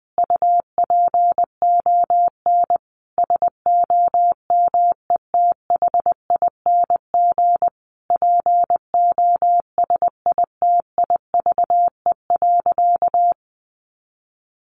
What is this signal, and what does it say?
Morse code, heard as a keyed tone, about 20 wpm, 700 Hz. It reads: UPON SOMETHING POSITIVE.
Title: Sentences from Top 1400 Words 20wpm